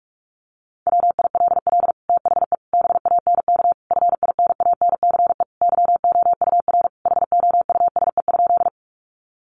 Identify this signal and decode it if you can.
Morse code, heard as a keyed tone, about 44 wpm, 700 Hz. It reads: WILL THE BANK FINANCE YOUR HOUSE?
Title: Sentences from Top 1500 Words 45wpm